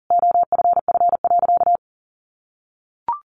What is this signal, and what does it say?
Morse code, heard as a keyed tone, about 40 wpm, 700 Hz. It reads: OFF.
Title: Sentences from Top 1300 Words 40wpm